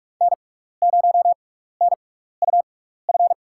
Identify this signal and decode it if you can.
Morse code, heard as a keyed tone, about 46 wpm, spaced as for 16 wpm, 700 Hz. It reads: N0NUF
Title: USA Call Signs - 5 Characters - 6x Character Spacing - 45wpm